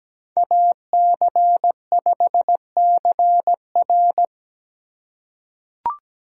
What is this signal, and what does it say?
Morse code, heard as a keyed tone, about 17 wpm, 700 Hz. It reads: AC5CR
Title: USA Call Signs - 5 Characters - 3x Character Spacing - 17wpm